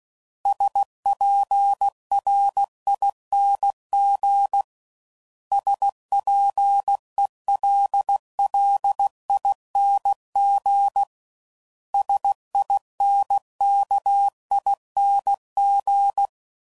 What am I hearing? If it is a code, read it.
Morse code, heard as a keyed tone, about 16 wpm, 800 Hz. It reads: SPRING SPELLING SINKING
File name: Suffixes / Special INGs / Sngl Ings